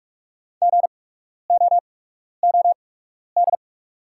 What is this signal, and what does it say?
Morse code, heard as a keyed tone, about 46 wpm, spaced as for 12 wpm, 690 Hz. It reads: GOOD
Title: 8x Character Spacing - Top 200 Words - 45wpm